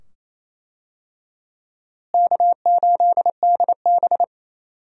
Morse code, heard as a keyed tone, about 27 wpm, 700 Hz. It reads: K8DB
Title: USA Call Signs - 4 Characters - Repeated 2x - 28wpm